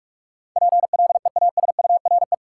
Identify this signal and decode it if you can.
Morse code, heard as a keyed tone, about 41 wpm, 700 Hz. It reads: PLEASURE
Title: Sentences from Top 1600 Words - New Word Drill Alt - 45wpm